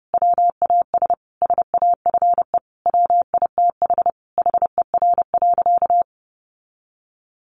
Morse code, heard as a keyed tone, about 30 wpm, 700 Hz. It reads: WAS SAFE WITH HER.
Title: Sentences from Top 900 Words 30wpm